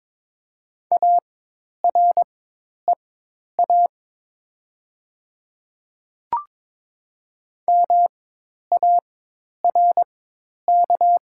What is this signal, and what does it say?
Morse code, heard as a keyed tone, about 22 wpm, spaced as for 10 wpm, 700 Hz. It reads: AREA MARK
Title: 4x Character Spacing - Top 500 Words - 22wpm